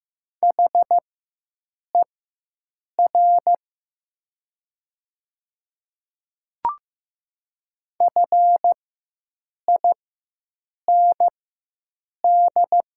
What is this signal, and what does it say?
Morse code, heard as a keyed tone, about 15 wpm, spaced as for 7 wpm, 700 Hz. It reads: HER FIND
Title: 4x Character Spacing - Top 100 Words - 15wpm